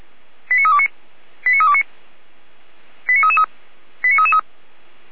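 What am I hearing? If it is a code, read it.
Five-tone selective call (ZVEI), heard as quick runs of five tones, tones of about 70 ms. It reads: 89329 89329 89333 89333
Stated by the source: ZVEI five-tone selective call 89337 received by SDR, recorded by Feuerwehr Orlamünde.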